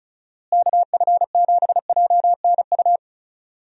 Morse code, heard as a keyed tone, about 36 wpm, 700 Hz. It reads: KF7JNU
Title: USA Call Signs - 6 Characters - Repeated 2x - 35wpm